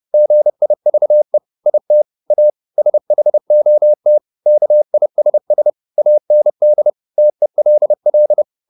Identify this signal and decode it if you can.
Morse code, heard as a keyed tone, about 30 wpm, 600 Hz. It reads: GIVE IT A SHOT KISS AND TELL